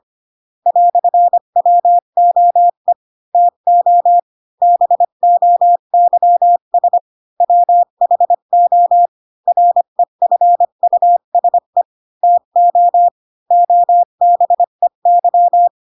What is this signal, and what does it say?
Morse code, heard as a keyed tone, about 25 wpm, 700 Hz. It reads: "WOE TO BOYS WHO REFUSE TO OBEY
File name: Pinocchio / Pinocchio ch4 20 25 ik1hge